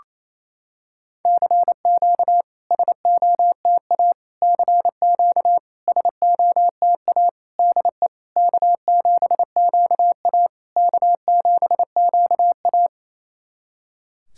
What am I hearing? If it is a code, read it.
Morse code, heard as a keyed tone, about 28 wpm, 700 Hz. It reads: CQ SOTA CQ SOTA DE K7QA K7QA
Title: SOTA QSO Practice 28wpm